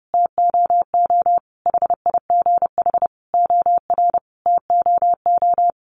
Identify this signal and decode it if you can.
Morse code, heard as a keyed tone, about 30 wpm, 700 Hz. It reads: TOO HIGH OR TOO
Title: Sentences from Top 1200 Words 30wpm